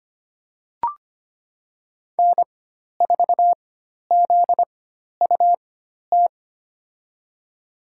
Morse code, heard as a keyed tone, about 25 wpm, spaced as for 12 wpm, 700 Hz. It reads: N4ZUT
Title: USA Call Signs - 5 Characters - 4x Character Spacing - 25wpm